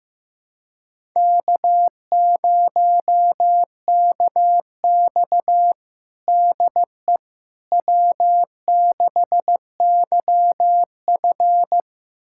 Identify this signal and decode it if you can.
Morse code, heard as a keyed tone, about 15 wpm, 700 Hz. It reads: K0KX DE W6YF
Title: USA Callsign DE Callsign - 4 Characters - 15wpm